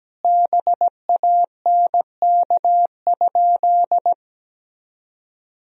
Morse code, heard as a keyed tone, about 17 wpm, 700 Hz. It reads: BANK?
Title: Sentences from Top 1000 Words - Review Part 4 - 17wpm